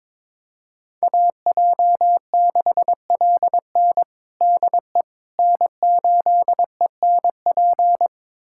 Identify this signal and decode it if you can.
Morse code, heard as a keyed tone, about 22 wpm, 700 Hz. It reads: AJ6LN DE N8ENP
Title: USA Callsign DE Callsign - 5 Characters - 22wpm